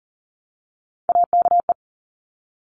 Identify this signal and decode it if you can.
Morse code, heard as a keyed tone, about 40 wpm, 700 Hz. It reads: AKE
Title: Top Three Letter Combinations in Words 40wpm